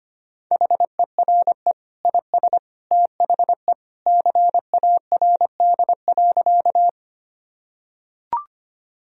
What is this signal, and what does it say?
Morse code, heard as a keyed tone, about 25 wpm, 700 Hz. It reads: HERE IS THE CARD.
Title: Sentences from Top 1000 Words - Review Part 4 - 25wpm